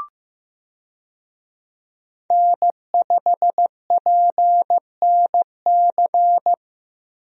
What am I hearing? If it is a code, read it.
Morse code, heard as a keyed tone, about 15 wpm, 700 Hz. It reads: N5PNC
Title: USA Call Signs - 5 Characters - 15wpm